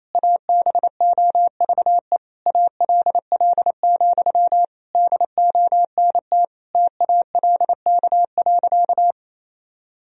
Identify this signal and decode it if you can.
Morse code, heard as a keyed tone, about 28 wpm, 700 Hz. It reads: ABOVE ALL, DONT TALK.
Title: Sentences from Top 1000 Words - Review Part 2 - 28wpm